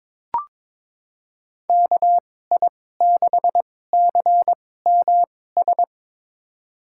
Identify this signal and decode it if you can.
Morse code, heard as a keyed tone, about 22 wpm, spaced as for 16 wpm, 700 Hz. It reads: KI6CMS
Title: USA Call Signs - 2x Character Spacing - 22wpm